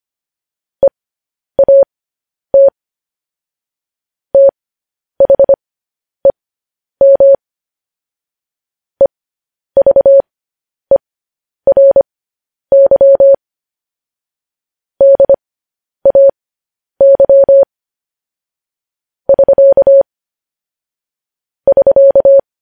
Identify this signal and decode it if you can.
Morse code, heard as a keyed tone, about 25 wpm, spaced as for 10 wpm, 550 Hz. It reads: EAT THEM EVERY DAY <SK> <SK>
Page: news headlines in morse code at 10 wpm